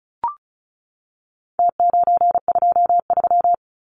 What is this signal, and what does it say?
Morse code, heard as a keyed tone, about 34 wpm, 700 Hz. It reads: T923
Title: Numbers - Contest Style - 4-Digit 35wpm